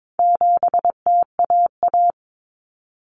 Morse code, heard as a keyed tone, about 22 wpm, 700 Hz. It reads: 7TAA
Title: Common Cut Numbers - 4-Digit 22wpm